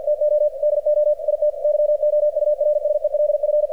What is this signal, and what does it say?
Morse code, heard as a keyed tone, about 48 wpm, 600 Hz, with radio noise. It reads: TO DO IT YOURSELF
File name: archived / CRSnet